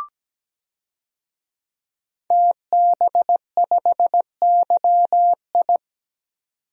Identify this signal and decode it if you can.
Morse code, heard as a keyed tone, about 17 wpm, 700 Hz. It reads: TB5YI
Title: DXCC-100 Call Signs - 5 Characters - 17wpm